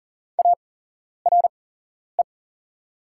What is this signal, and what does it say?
Morse code, heard as a keyed tone, about 40 wpm, spaced as for 11 wpm, 700 Hz. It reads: ARE
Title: 8x Character Spacing - Top 300 Words - 40wpm